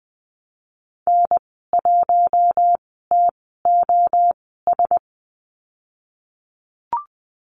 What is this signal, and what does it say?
Morse code, heard as a keyed tone, about 20 wpm, spaced as for 15 wpm, 700 Hz. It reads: N1TOS
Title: 042 Lesson 8g - Number 1 - Call Signs 20wpm